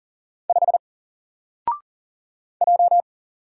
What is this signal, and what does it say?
Morse code, heard as a keyed tone, about 40 wpm, 700 Hz. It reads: 5 J
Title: Single Letter-Number 40wpm